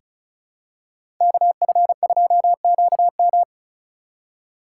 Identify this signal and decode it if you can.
Morse code, heard as a keyed tone, about 34 wpm, 700 Hz. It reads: KF2QM
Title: USA Call Signs - 3x Character Spacing - 35wpm